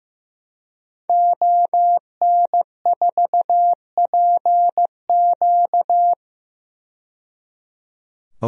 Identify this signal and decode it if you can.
Morse code, heard as a keyed tone, about 15 wpm, 700 Hz. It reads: ON4PQ
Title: DXCC-100 Call Signs - 5 Characters - 15wpm